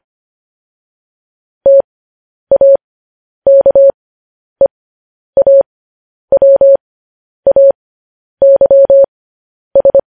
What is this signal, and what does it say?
Morse code, heard as a keyed tone, about 25 wpm, spaced as for 10 wpm, 550 Hz. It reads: TAKEAWAYS
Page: news-headlines-in-morse-code-at-10-wpm